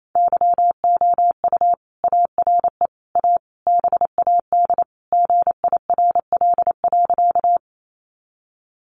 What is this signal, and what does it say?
Morse code, heard as a keyed tone, about 28 wpm, 700 Hz. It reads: YOU ARE A BAD GIRL.
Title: Sentences from Top 700 Words 28wpm